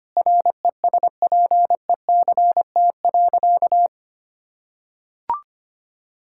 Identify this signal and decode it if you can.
Morse code, heard as a keyed tone, about 25 wpm, 700 Hz. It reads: RESPECT.
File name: practice / 25wpm / Sentences with 4-Word Phrases - 25wpm